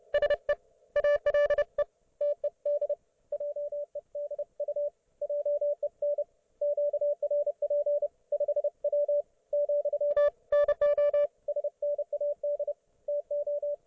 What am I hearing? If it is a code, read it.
Morse code, heard as a keyed tone, about 31 wpm, 580 Hz. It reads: SE ALE TED JEDU JEN QRP 5W , NO SNAD TO